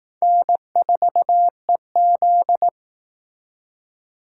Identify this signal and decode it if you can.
Morse code, heard as a keyed tone, about 18 wpm, 700 Hz. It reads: N4EZ
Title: Speed Racing - USA 1x2 Call Signs 15wpm